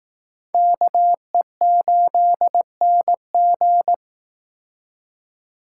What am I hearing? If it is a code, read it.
Morse code, heard as a keyed tone, about 19 wpm, 700 Hz. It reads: KE8NG